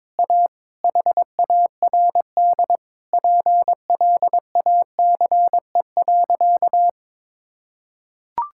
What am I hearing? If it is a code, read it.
Morse code, heard as a keyed tone, about 22 wpm, 700 Hz. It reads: A HARD PLACE.
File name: Sentences with 7-Word Phrases - 22wpm